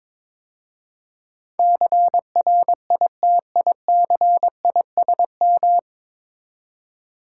Morse code, heard as a keyed tone, about 22 wpm, 700 Hz. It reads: CRITICISM
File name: practice / 22wpm / 9 Letter Words - First 3 Letters Given - 22wpm